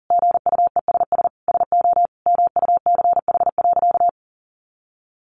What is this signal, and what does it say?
Morse code, heard as a keyed tone, about 40 wpm, 700 Hz. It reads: GUESS SO MUCH.
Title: Sentences from Top 1500 Words 40wpm